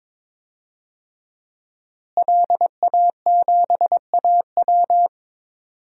Morse code, heard as a keyed tone, about 21 wpm, 700 Hz. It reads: LA7AW
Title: DXCC-100 Call Signs - 5 Characters - Repeated 3x - 22wpm